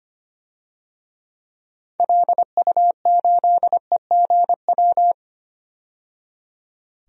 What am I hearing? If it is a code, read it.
Morse code, heard as a keyed tone, about 25 wpm, 700 Hz. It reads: LU8EGW
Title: DXCC-100 Call Signs - 6 Characters - 25wpm